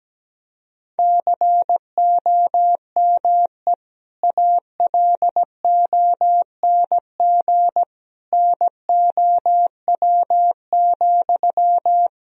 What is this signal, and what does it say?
Morse code, heard as a keyed tone, about 17 wpm, 700 Hz. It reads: COME ALONG NOW,
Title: Sentences from Top 1000 Words - Review Part 4 - 17wpm